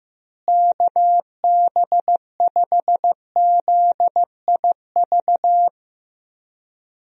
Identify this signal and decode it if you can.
Morse code, heard as a keyed tone, about 15 wpm, 700 Hz. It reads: KB5ZIV
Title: USA Call Signs - 6 Characters - Repeated 3x - 15wpm